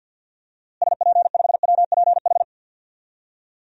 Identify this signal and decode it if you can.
Morse code, heard as a keyed tone, about 47 wpm, 700 Hz. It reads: SP5WPH
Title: DXCC-100 Call Signs - 6 Characters - Repeated 3x - 50wpm